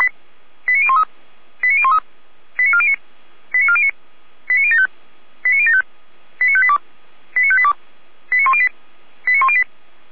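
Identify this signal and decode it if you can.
Five-tone selective call (ZVEI), heard as quick runs of five tones, tones of about 70 ms. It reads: 89013 89013 89409 89409 89075 89075 89582 89582 89108 89108
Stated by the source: ZVEI five-tone selective call 89370 received on SDR.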